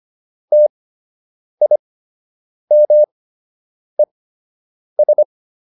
Morse code, heard as a keyed tone, about 25 wpm, spaced as for 8 wpm, 600 Hz. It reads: TIMES